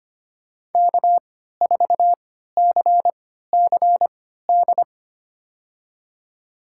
Morse code, heard as a keyed tone, about 25 wpm, spaced as for 14 wpm, 700 Hz. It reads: K4CCD
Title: USA Call Signs - 5 Characters - 3x Character Spacing - 25wpm